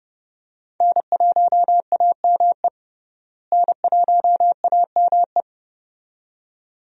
Morse code, heard as a keyed tone, about 31 wpm, 700 Hz. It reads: N1AME N1AME
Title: USA Call Signs - 5 Characters - Repeated 2x - 30wpm